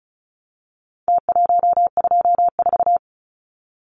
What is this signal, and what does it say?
Morse code, heard as a keyed tone, about 34 wpm, 700 Hz. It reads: T124
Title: Numbers - Contest Style - 4-Digit 35wpm